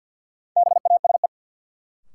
Morse code, heard as a keyed tone, about 48 wpm, 700 Hz. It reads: BASE
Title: Sentences from Top 1600 Words - New Word Drill Alt - 50wpm